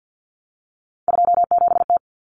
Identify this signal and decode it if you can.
Morse code, heard as a keyed tone, about 50 wpm, 700 Hz. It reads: 27T
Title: Common Cut Numbers - 3-Digit 50wpm